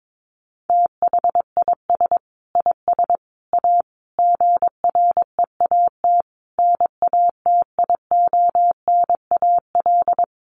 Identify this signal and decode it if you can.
Morse code, heard as a keyed tone, about 22 wpm, 700 Hz. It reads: THIS IS A GREAT NATIONAL
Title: Sentences from Top 1300 Words 22wpm